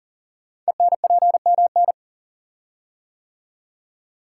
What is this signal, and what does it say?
Morse code, heard as a keyed tone, about 40 wpm, 700 Hz. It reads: ENPMN
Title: Code Groups - 5 Letter:Numbers - 40wpm